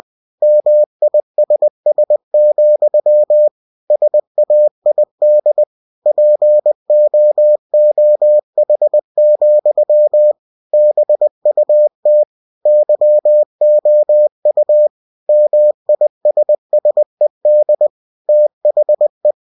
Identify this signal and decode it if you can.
Morse code, heard as a keyed tone, about 20 wpm, 600 Hz. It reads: MISS, SAID POOH, BUT YOU MISSED THE